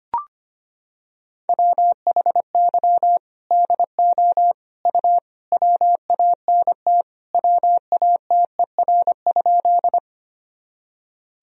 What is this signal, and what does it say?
Morse code, heard as a keyed tone, about 25 wpm, 700 Hz. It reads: WHY DO U WANT WATER?
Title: Sentences from Top 200 Words and Abbreviations 25wpm